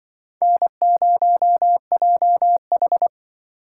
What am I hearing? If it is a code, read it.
Morse code, heard as a keyed tone, about 24 wpm, 700 Hz. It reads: N0JH
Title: Speed Racing - USA Call Signs 20wpm